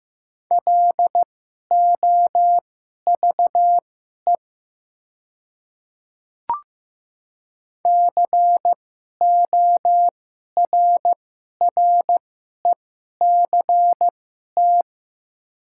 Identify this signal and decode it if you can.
Morse code, heard as a keyed tone, about 15 wpm, spaced as for 11 wpm, 700 Hz. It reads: LOVE CORRECT
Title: 2x Character Spacing - Top 500 Words - 15wpm